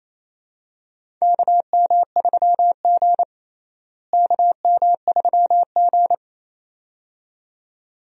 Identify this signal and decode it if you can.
Morse code, heard as a keyed tone, about 28 wpm, 700 Hz. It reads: KM3G KM3G
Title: USA Call Signs - 4 Characters - Repeated 2x - 28wpm